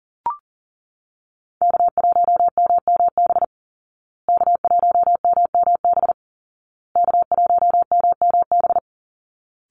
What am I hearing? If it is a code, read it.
Morse code, heard as a keyed tone, about 37 wpm, 700 Hz. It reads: K1MMB K1MMB K1MMB
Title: Call Signs Repeated 2x - 40wpm